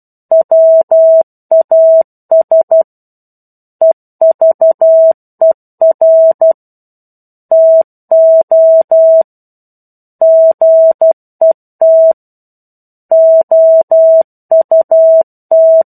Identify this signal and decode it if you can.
Morse code, heard as a keyed tone, about 12 wpm, 650 Hz. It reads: WAS EVER TO GET OUT